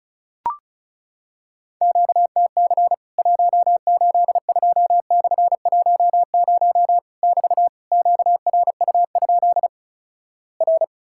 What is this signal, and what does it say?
Morse code, heard as a keyed tone, about 36 wpm, 700 Hz. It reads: QTC 182/10 <BT> QRU? R
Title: Worked All Europe DX Contest - QTC Exchange - 35wpm